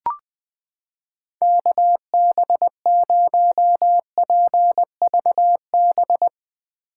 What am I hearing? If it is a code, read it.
Morse code, heard as a keyed tone, about 20 wpm, 700 Hz. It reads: KB0PVB
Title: 2023 POTA Activator Call Signs - 20wpm